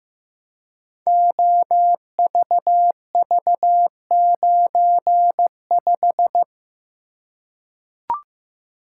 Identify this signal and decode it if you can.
Morse code, heard as a keyed tone, about 15 wpm, 700 Hz. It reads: OVV95
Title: Code Groups - 5 Letter:Numbers - 15wpm